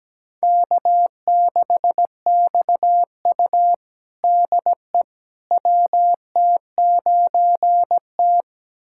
Morse code, heard as a keyed tone, about 17 wpm, 700 Hz. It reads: K6XU DE WT9T
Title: USA Callsign DE Callsign - 4 Characters - 17wpm